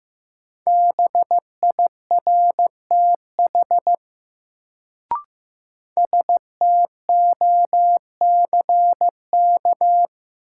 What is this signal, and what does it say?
Morse code, heard as a keyed tone, about 15 wpm, 700 Hz. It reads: BIRTH STOCK
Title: Sentences from Top 800 Words - New Word Drill - 15wpm